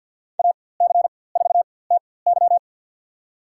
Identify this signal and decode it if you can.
Morse code, heard as a keyed tone, about 52 wpm, spaced as for 23 wpm, 700 Hz. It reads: AC4TY
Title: USA Call Signs - 5 Characters - 4x Character Spacing - 50wpm